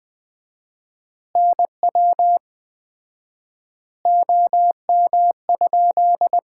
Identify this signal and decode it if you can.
Morse code, heard as a keyed tone, about 20 wpm, 700 Hz. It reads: NW OM?
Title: Common QSO Phrases - 4x Word-Spacing - 20wpm